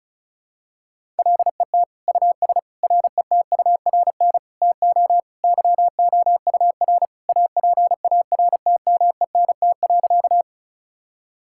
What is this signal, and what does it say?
Morse code, heard as a keyed tone, about 36 wpm, 690 Hz. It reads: LET US RETURN TO YOUR APARTMENT.
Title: Sentences from Top 1000 Words - Review Part 6 - 35wpm